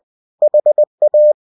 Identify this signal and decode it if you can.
Morse code, heard as a keyed tone, about 20 wpm, 600 Hz. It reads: HA
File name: cw-books / winnie-the-pooh / winnie-20wpm-page-0047